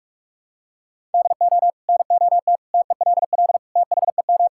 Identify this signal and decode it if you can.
Morse code, heard as a keyed tone, about 41 wpm, 700 Hz. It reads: DO NOT TELL THEM
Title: Sentences from Top 1000 Words - Review Part 6 - 45wpm